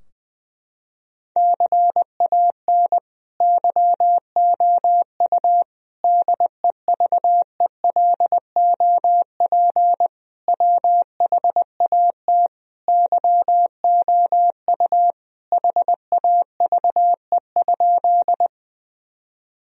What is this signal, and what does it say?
Morse code, heard as a keyed tone, about 20 wpm, 700 Hz. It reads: CAN YOU DEVELOP WHAT YOU HAVE?
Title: Sentences from Top 1000 Words - Review Part 2 - 20wpm